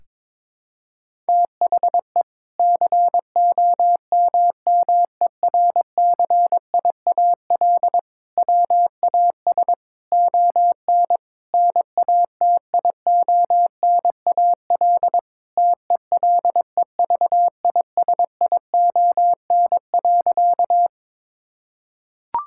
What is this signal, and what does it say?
Morse code, heard as a keyed tone, about 22 wpm, 700 Hz. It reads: THE COMMERCIAL WAS ON NATIONAL TELEVISION.
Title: Sentences from Top 1000 Words - Review Part 5 - 22wpm